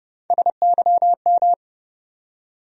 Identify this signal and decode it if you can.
Morse code, heard as a keyed tone, about 30 wpm, 700 Hz. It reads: SYM
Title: USA Call Sign Postfixes - 3 Letters 30wpm